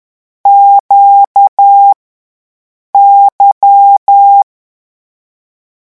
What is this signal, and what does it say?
Morse code, heard as a keyed tone, about 10 wpm, 800 Hz. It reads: Q Y
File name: morse mp3 / lecon2-vitesse 10